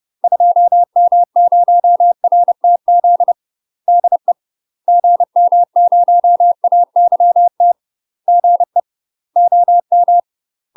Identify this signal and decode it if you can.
Morse code, heard as a keyed tone, about 30 wpm, 700 Hz. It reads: 2M0RTZ DE GM0AYT GE OM
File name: UK MASTER QSOs TRAINING / 30WPM / UK MASTER QSOs 0044